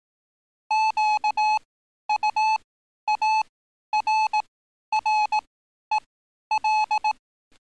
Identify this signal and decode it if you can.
Morse code, heard as a keyed tone, about 18 wpm, spaced as for 11 wpm, 860 Hz. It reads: QUARREL
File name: training / NAVPERS 11049to11060 / NAVPERS 11051 RA RB